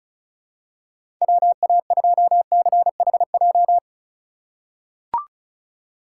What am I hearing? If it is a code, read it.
Morse code, heard as a keyed tone, about 36 wpm, 700 Hz. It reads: WA2CHJ
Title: USA Call Signs - 6 Characters - 4x Character Spacing - 35wpm